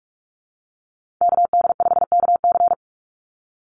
Morse code, heard as a keyed tone, about 46 wpm, 700 Hz. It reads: KD5KC
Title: NA SOTA Chaser Callsigns 45wpm